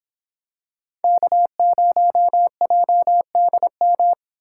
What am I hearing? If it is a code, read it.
Morse code, heard as a keyed tone, about 26 wpm, 700 Hz. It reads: K0JDM